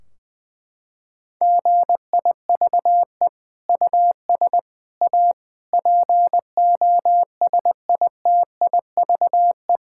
Morse code, heard as a keyed tone, about 20 wpm, 700 Hz. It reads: GIVE US A POSITIVE
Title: Sentences from Top 1600 Words 20wpm